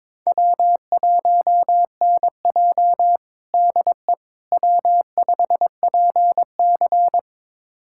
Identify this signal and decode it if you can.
Morse code, heard as a keyed tone, about 21 wpm, 700 Hz. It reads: W1NJ DE W5PC
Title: USA Callsign DE Callsign - 4 Characters - 22wpm